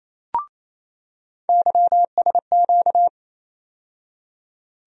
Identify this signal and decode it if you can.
Morse code, heard as a keyed tone, about 28 wpm, 700 Hz. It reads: YSQ
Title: USA Call Sign Postfixes - 3 Letters 28wpm